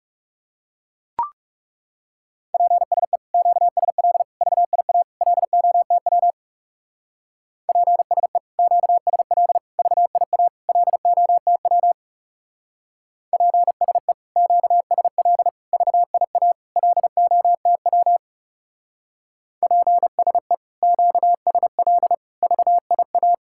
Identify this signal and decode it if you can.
Morse code, heard as a keyed tone, about 39 wpm, 700 Hz. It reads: PSE QSL VIA LOTW PSE QSL VIA LOTW PSE QSL VIA LOTW PSE QSL VIA